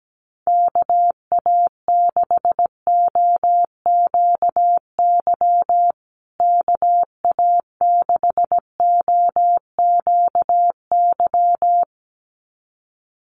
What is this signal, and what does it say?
Morse code, heard as a keyed tone, about 17 wpm, 700 Hz. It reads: KA6OQY KA6OQY
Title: USA Call Signs Repeated 2x 17wpm